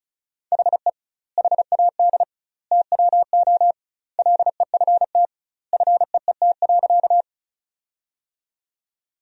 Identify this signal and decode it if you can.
Morse code, heard as a keyed tone, about 36 wpm, 700 Hz. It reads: HE HAD TWO LEFT FEET.
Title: 2x Word-Spacing - Sentences from Top 300 Words - 35wpm